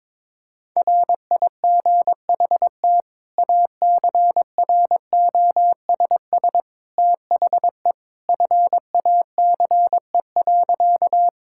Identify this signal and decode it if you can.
Morse code, heard as a keyed tone, about 22 wpm, 700 Hz. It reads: RIGHT ACROSS THE FACE.
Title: Sentences from Top 1000 Words - Review Part 5 - 22wpm